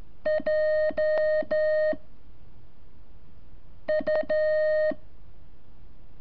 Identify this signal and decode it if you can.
Morse code, heard as a keyed tone, about 13 wpm, 620 Hz. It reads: J U
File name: curso / ejercicio 26